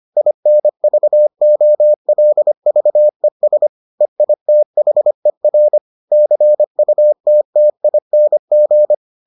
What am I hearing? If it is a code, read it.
Morse code, heard as a keyed tone, about 25 wpm, 600 Hz. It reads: INVOLVES EITHER CUTTING